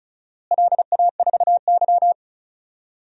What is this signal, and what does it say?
Morse code, heard as a keyed tone, about 34 wpm, 700 Hz. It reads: LA4Y
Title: DXCC-100 Call Signs - 4 Characters - Repeated 3x - 35wpm